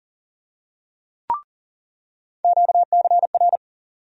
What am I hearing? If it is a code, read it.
Morse code, heard as a keyed tone, about 39 wpm, 700 Hz. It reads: QCR
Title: USA Call Sign Postfixes - 3 Letters 40wpm